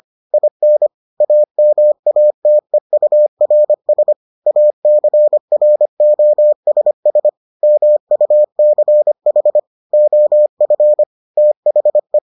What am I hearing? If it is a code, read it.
Morse code, heard as a keyed tone, about 25 wpm, 600 Hz. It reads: IN AMATEURS ACROSS MUCH OF THE